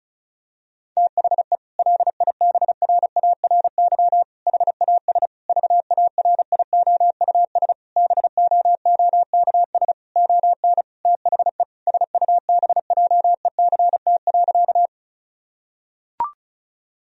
Morse code, heard as a keyed tone, about 36 wpm, 700 Hz. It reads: THE LIBRARY HAS VARIOUS BOOKS ON THE SUBJECT.
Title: Sentences from Top 1000 Words - Review Part 4 - 35wpm